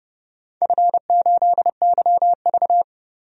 Morse code, heard as a keyed tone, about 30 wpm, 700 Hz. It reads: F8YV
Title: DXCC-100 Call Signs - 4 Characters - 30wpm